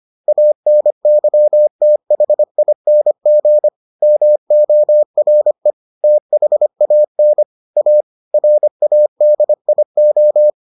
Morse code, heard as a keyed tone, about 25 wpm, 600 Hz. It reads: ANYTHING MORE THAN A RADIO